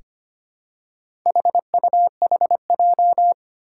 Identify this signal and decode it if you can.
Morse code, heard as a keyed tone, about 25 wpm, 700 Hz. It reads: HUHJ
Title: Four Challenging Letter-Number Combinations 25wpm